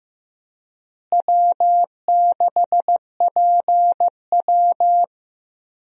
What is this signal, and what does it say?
Morse code, heard as a keyed tone, about 15 wpm, 700 Hz. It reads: W6PW
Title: USA Call Signs - 4 Characters - Repeated 2x - 15wpm